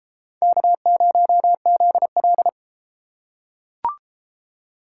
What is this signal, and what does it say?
Morse code, heard as a keyed tone, about 30 wpm, 700 Hz. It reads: K0ZL